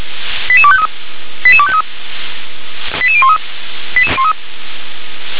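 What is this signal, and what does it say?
Five-tone selective call (ZVEI), heard as quick runs of five tones, tones of about 70 ms. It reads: 88253 88253 88013
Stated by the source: ZVEI five-tone selective call received by SDR.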